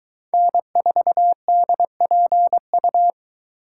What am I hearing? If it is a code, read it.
Morse code, heard as a keyed tone, about 22 wpm, 700 Hz. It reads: N4DPU